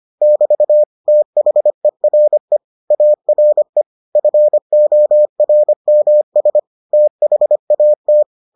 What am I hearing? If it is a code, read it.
Morse code, heard as a keyed tone, about 25 wpm, 600 Hz. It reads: <BT> THERE ARE FORMS THAT